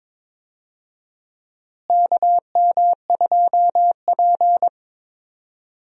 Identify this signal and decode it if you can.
Morse code, heard as a keyed tone, about 21 wpm, 700 Hz. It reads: KM2P
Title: USA Call Signs - 4 Characters - Repeated 3x - 22wpm